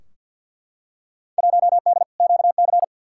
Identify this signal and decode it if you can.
Morse code, heard as a keyed tone, about 52 wpm, 700 Hz. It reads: 1D QC
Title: Field Day Exchanges - 50wpm